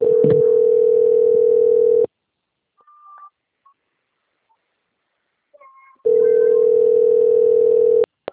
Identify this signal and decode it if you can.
Ringback tone (North America), 440 + 480 Hz.
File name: out- - - - - .10535